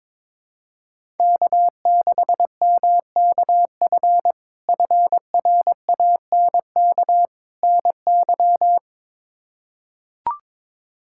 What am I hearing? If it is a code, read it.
Morse code, heard as a keyed tone, about 22 wpm, 700 Hz. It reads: K6MKF FRANK NY
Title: K1USN SST QSO Practice - 22wpm